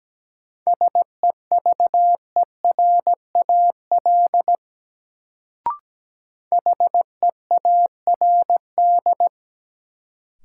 Morse code, heard as a keyed tone, about 17 wpm, 700 Hz. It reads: SEVERAL HEARD
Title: Sentences from Top 400 Words - New Word Drill - 17wpm